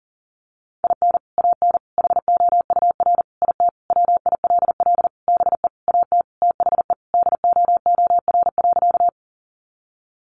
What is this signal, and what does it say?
Morse code, heard as a keyed tone, about 37 wpm, 700 Hz. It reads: IN AN HOUR IT WILL BE AT THE DOOR.
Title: Sentences from Top 400 Words 40wpm